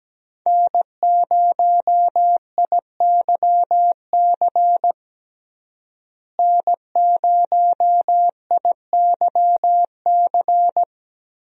Morse code, heard as a keyed tone, about 17 wpm, 700 Hz. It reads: N0IYC N0IYC
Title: USA Call Signs - 5 Characters - Repeated 3x - 17wpm